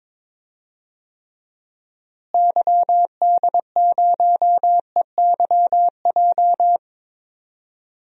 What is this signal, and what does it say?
Morse code, heard as a keyed tone, about 22 wpm, 700 Hz. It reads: YD0EYJ
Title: DXCC-100 Call Signs - 6 Characters - Repeated 2x - 22wpm